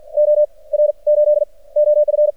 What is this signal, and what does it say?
Morse code, heard as a keyed tone, about 53 wpm, 600 Hz. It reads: O R 8 OF
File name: archived / CRSnet